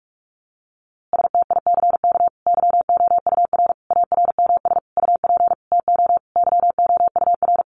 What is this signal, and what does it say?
Morse code, heard as a keyed tone, about 46 wpm, 700 Hz. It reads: STICK YOUR ARMS UP TO YOUR
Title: A Sound Of Thunder 45wpm